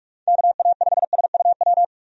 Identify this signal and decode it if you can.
Morse code, heard as a keyed tone, about 45 wpm, 700 Hz. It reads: KA5SUW
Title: USA Call Signs - 6 Characters - 45wpm